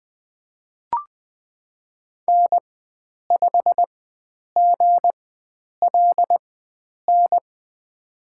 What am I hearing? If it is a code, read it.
Morse code, heard as a keyed tone, about 20 wpm, spaced as for 9 wpm, 700 Hz. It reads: N5GLN
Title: USA Call Signs - 5 Characters - 4x Character Spacing - 20wpm